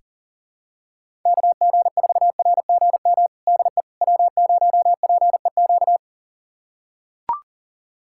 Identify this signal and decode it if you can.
Morse code, heard as a keyed tone, about 40 wpm, 700 Hz. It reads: KG4RGM DE W0PEQ
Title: USA Callsign DE Callsign 40wpm